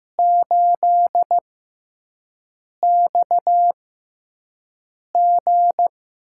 Morse code, heard as a keyed tone, about 15 wpm, spaced as for 5 wpm, 700 Hz. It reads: 8XG
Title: USA Call Signs - 6x Character Spacing - 15wpm